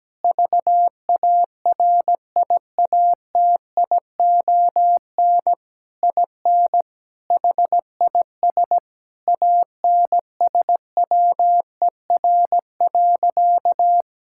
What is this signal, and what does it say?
Morse code, heard as a keyed tone, about 17 wpm, 700 Hz. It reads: VARIATION IN HIS ANSWER.
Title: Sentences from Top 1600 Words 17wpm